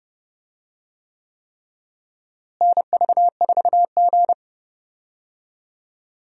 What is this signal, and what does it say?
Morse code, heard as a keyed tone, about 31 wpm, 700 Hz. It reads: NV4G
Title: USA Call Signs - 4 Characters - Repeated 3x - 30wpm